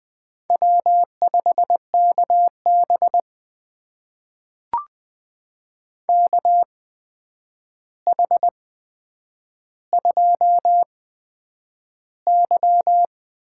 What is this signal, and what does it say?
Morse code, heard as a keyed tone, about 20 wpm, spaced as for 9 wpm, 700 Hz. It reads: W5KB KH2Y
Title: USA Call Signs - 4 Characters - 8x Character Spacing - 20wpm